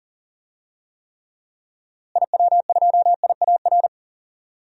Morse code, heard as a keyed tone, about 40 wpm, 700 Hz. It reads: IW2IAR
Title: DXCC-100 Call Signs - 6 Characters - Repeated 2x - 40wpm